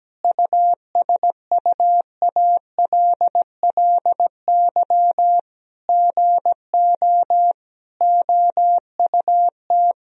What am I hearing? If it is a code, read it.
Morse code, heard as a keyed tone, about 17 wpm, 700 Hz. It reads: USUALLY GO OUT
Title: Sentences from Top 1000 Words - Review Part 3 - 17wpm